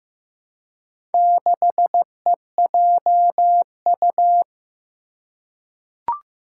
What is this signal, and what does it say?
Morse code, heard as a keyed tone, about 15 wpm, 700 Hz. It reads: 6EJU
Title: Four Challenging Letter-Number Combinations 15wpm